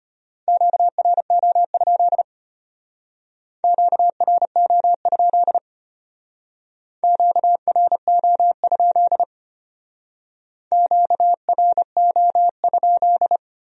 Morse code, heard as a keyed tone, about 32 wpm, 700 Hz. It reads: QRO? QRO? QRO? QRO?